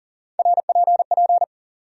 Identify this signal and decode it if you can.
Morse code, heard as a keyed tone, about 40 wpm, 700 Hz. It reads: RPP
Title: USA Call Sign Postfixes - 3 Letters 40wpm